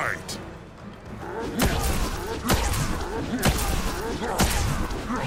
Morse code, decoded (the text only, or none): none